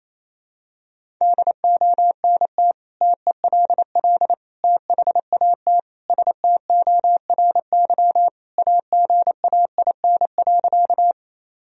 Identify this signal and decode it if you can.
Morse code, heard as a keyed tone, about 28 wpm, 700 Hz. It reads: DONT TELL THAT STORY AGAIN.
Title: Sentences from Top 1000 Words - Review Part 1 - 28wpm